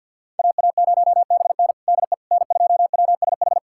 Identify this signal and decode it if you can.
Morse code, heard as a keyed tone, about 51 wpm, 690 Hz. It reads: AA0BN DE N2WSH
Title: USA Callsign DE Callsign - 5 Characters - 50wpm